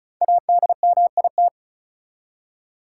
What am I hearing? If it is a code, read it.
Morse code, heard as a keyed tone, about 34 wpm, 700 Hz. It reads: ADMIT
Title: Five Letter Words - 35wpm